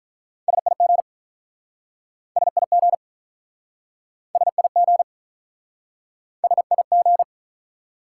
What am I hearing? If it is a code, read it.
Morse code, heard as a keyed tone, about 48 wpm, 700 Hz. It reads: SIG SIG SIG S I G